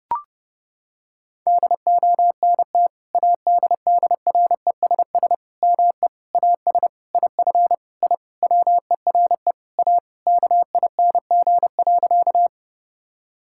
Morse code, heard as a keyed tone, about 32 wpm, 700 Hz. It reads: DONT ADDRESS ME AS IF I WERE A KING.
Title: Sentences from Top 1000 Words - Review Part 6 - 30wpm